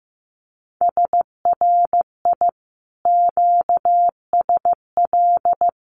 Morse code, heard as a keyed tone, about 15 wpm, 700 Hz. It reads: SRI QSL
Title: Common QSO Phrases Encore 15wpm